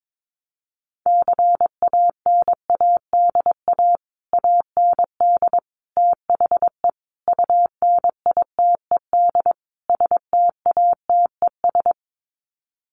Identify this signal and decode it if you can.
Morse code, heard as a keyed tone, about 22 wpm, 700 Hz. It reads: CANADA AND THE UNITED STATES
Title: Sets of 5 Words Encore 22wpm